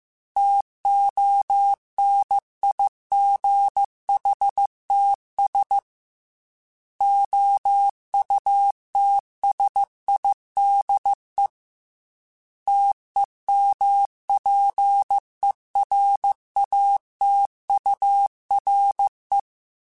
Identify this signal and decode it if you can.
Morse code, heard as a keyed tone, about 15 wpm, 800 Hz. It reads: TONIGHTS OUTSIDE TEMPERATURE